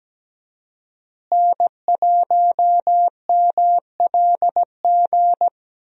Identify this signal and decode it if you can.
Morse code, heard as a keyed tone, about 17 wpm, 700 Hz. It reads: N1MLG
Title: USA Call Signs - 5 Characters - Repeated 2x - 17wpm